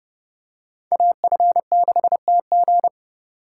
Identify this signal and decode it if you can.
Morse code, heard as a keyed tone, about 29 wpm, 700 Hz. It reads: AF6TG